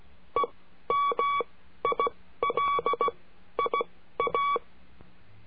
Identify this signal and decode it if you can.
Morse code, heard as a keyed tone, about 16 wpm, spaced as for 12 wpm, 1130 Hz. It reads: EMILIA
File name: cursos / CURSO DE TELEGRAFIA / Telegrafia - Trilha 22